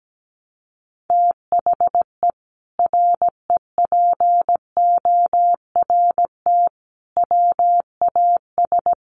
Morse code, heard as a keyed tone, about 17 wpm, 700 Hz. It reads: THE REPORT WAS
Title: Sentences from Top 900 Words 17wpm